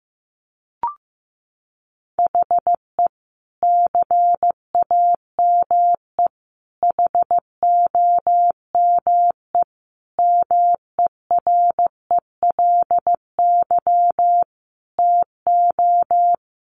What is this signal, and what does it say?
Morse code, heard as a keyed tone, about 15 wpm, 700 Hz. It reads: HE CAME HOME MERELY TO
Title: Sentences from Top 1400 Words 15wpm